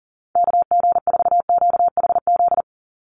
Morse code, heard as a keyed tone, about 40 wpm, 700 Hz. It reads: KG4QHZ
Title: USA Call Signs 40wpm